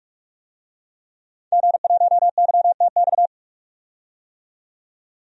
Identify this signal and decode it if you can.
Morse code, heard as a keyed tone, about 46 wpm, 700 Hz. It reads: G1YTX
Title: DXCC-100 Call Signs - 5 Characters - 45wpm